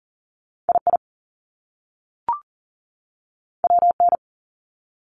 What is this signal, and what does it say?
Morse code, heard as a keyed tone, about 38 wpm, 700 Hz. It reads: II WN
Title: Top Two Letter Combinations in Words 40wpm